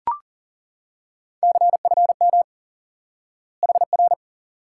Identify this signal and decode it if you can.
Morse code, heard as a keyed tone, about 40 wpm, 700 Hz. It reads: CFM HR
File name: IQE Course 3 - Lesson 89 - YAGI - Two Elements - 40wpm